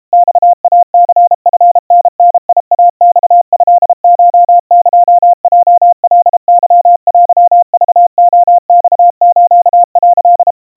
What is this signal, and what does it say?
Morse code (Japanese Wabun code), heard as a keyed tone, about 32 wpm, 700 Hz. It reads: マイニチタダイマトコエヲカケテクレマス」
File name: kawa / Morse sample / kamyu 120 cpm 700 Hz Crispy